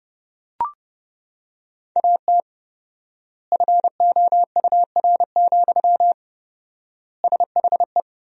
Morse code, heard as a keyed tone, about 32 wpm, 700 Hz. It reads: AT FOUR, SHE
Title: 4x Word-Spacing - Sentences from Top 300 Words - 30wpm